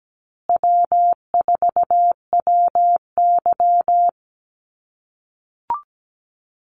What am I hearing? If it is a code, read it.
Morse code, heard as a keyed tone, about 17 wpm, 700 Hz. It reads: W4WY
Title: USA Call Signs Repeated 3x 17wpm